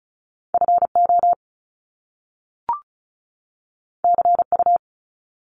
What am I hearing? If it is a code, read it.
Morse code, heard as a keyed tone, about 36 wpm, 700 Hz. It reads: FO CU
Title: Top Two Letter Combinations in Words 35wpm